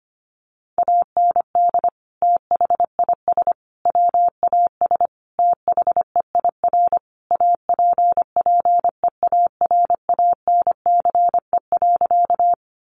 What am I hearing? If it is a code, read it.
Morse code, heard as a keyed tone, about 25 wpm, 700 Hz. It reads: AND THIS WAS THEIR APPEARANCE.
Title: Sentences from Top 1100 Words 25wpm